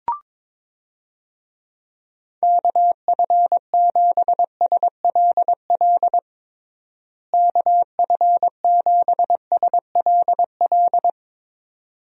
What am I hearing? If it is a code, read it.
Morse code, heard as a keyed tone, about 22 wpm, 700 Hz. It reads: KF7SLL KF7SLL
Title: USA Call Signs - 6 Characters - Repeated 2x - 22wpm